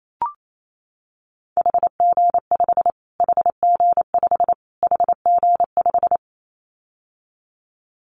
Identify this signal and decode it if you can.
Morse code, heard as a keyed tone, about 28 wpm, 700 Hz. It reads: HG5 HG5 HG5
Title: Callsign Prefixes for Top 25 DXCC - Repeated 3x - 28wpm